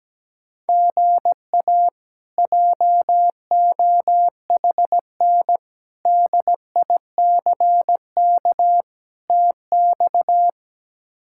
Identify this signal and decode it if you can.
Morse code, heard as a keyed tone, about 17 wpm, 700 Hz. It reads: GA JOHN DICK TX
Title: K1USN SST QSO Practice - 17wpm